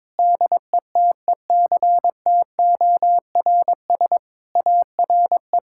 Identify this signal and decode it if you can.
Morse code, heard as a keyed tone, about 22 wpm, 700 Hz. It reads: DETECTORS ARE
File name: Difficult Sentences with Hints - 22wpm